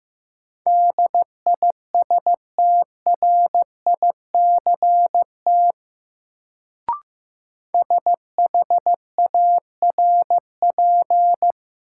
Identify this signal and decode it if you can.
Morse code, heard as a keyed tone, about 15 wpm, 700 Hz. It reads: DISTRICT SHARP
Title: Sentences from Top 1600 Words - New Word Drill Alt - 15wpm